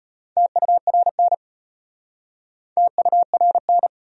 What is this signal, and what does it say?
Morse code, heard as a keyed tone, about 33 wpm, 700 Hz. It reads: TURN TURN